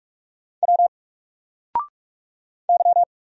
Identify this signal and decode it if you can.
Morse code, heard as a keyed tone, about 46 wpm, 700 Hz. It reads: W Y
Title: Single Letter-Number Alt 45wpm